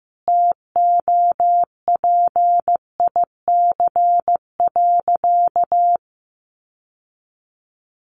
Morse code, heard as a keyed tone, about 15 wpm, 700 Hz. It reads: TOPIC.
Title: Sentences from Top 1200 Words 15wpm